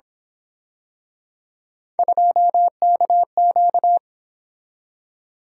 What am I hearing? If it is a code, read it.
Morse code, heard as a keyed tone, about 27 wpm, 700 Hz. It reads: 2KQ
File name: Speed-Racing - Three Letter-Number Combinations - 22wpm